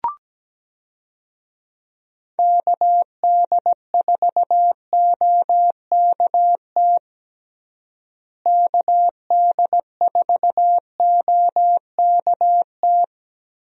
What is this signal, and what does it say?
Morse code, heard as a keyed tone, about 17 wpm, 700 Hz. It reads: KD4OKT KD4OKT
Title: USA Call Signs - 6 Characters - Repeated 3x - 17wpm